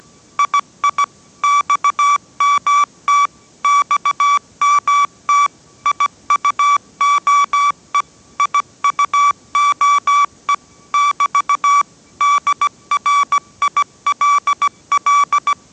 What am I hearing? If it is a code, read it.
Morse code, heard as a keyed tone, about 17 wpm, 1150 Hz. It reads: II XMT XMT IUOE IUOE <BT> DRILL
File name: 18wpm